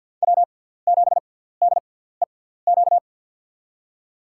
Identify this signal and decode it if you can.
Morse code, heard as a keyed tone, about 49 wpm, spaced as for 17 wpm, 700 Hz. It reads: W7DEQ
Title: USA Call Signs - 5 Characters - 6x Character Spacing - 50wpm